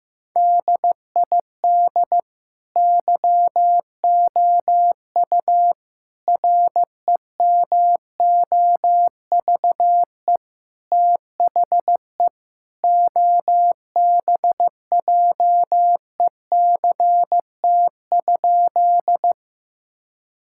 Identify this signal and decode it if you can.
Morse code, heard as a keyed tone, about 15 wpm, 700 Hz. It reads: DID YOU REMOVE THE OBJECT?
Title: Sentences from Top 1000 Words - Review Part 5 - 15wpm